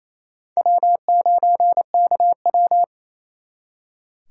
Morse code, heard as a keyed tone, about 35 wpm, 700 Hz. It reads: W9KW
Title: Speed Racing - USA 1x2 Call Signs 28wpm